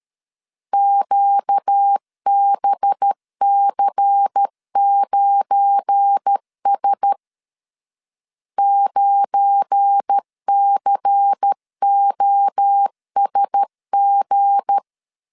Morse code, heard as a keyed tone, about 13 wpm, 800 Hz. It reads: QBC9S 9COSG
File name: LEZIONI / 7 GF9Z / 7 GF9 12wpm 2319